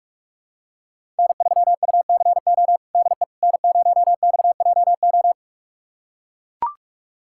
Morse code, heard as a keyed tone, about 42 wpm, 700 Hz. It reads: N3UCO DE N0XJO
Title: USA Callsign DE Callsign - 5 Characters - 45wpm